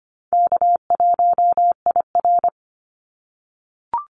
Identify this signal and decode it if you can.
Morse code, heard as a keyed tone, about 25 wpm, 700 Hz. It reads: K1IR
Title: Call Signs - 25wpm